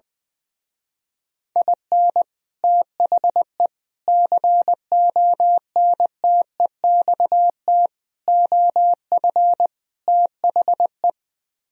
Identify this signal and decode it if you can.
Morse code, heard as a keyed tone, about 20 wpm, 700 Hz. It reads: IN THE CONTEXT OF THE
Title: Speed-Racing - Sets of 5 Words 20wpm